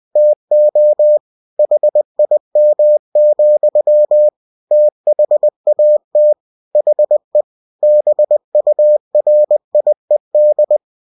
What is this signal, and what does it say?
Morse code, heard as a keyed tone, about 20 wpm, 600 Hz. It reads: TO HIM, THAT HE BURIED